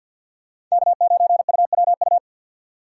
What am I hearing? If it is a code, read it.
Morse code, heard as a keyed tone, about 51 wpm, 700 Hz. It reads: K9UWU
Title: USA Call Signs - 5 Characters - 4x Character Spacing - 50wpm